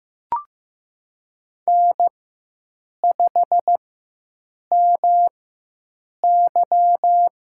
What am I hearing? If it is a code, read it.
Morse code, heard as a keyed tone, about 15 wpm, spaced as for 7 wpm, 700 Hz. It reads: N5MY
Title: USA Call Signs - 4x Character Spacing - 15wpm